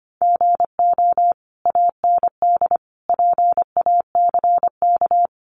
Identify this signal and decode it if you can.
Morse code, heard as a keyed tone, about 25 wpm, 700 Hz. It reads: GO AND PACK
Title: Sentences from Top 1300 Words 25wpm